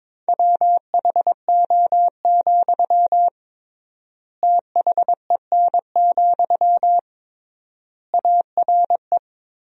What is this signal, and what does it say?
Morse code, heard as a keyed tone, about 22 wpm, 700 Hz. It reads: WHO, THEN, ARE
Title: 3x Word-Spacing - Sentences from Top 100 Words - 22wpm